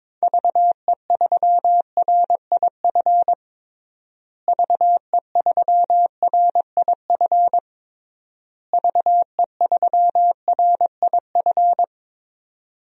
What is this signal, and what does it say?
Morse code, heard as a keyed tone, about 22 wpm, 700 Hz. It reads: VE3RIF VE3RIF VE3RIF
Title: DXCC-100 Call Signs - 6 Characters - Repeated 3x - 22wpm